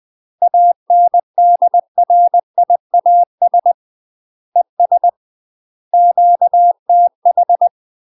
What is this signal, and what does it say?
Morse code, heard as a keyed tone, about 20 wpm, 700 Hz. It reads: ANDRIAS ES QTH
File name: UK MASTER QSOs TRAINING/20WPM